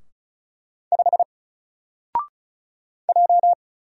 Morse code, heard as a keyed tone, about 36 wpm, 700 Hz. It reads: 5 J
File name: Single-Letter-Number-35wpm